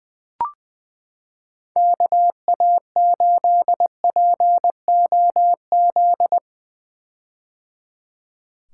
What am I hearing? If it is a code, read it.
Morse code, heard as a keyed tone, about 20 wpm, 700 Hz. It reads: KA8POZ
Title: K1USN SST QSO Practice - 20wpm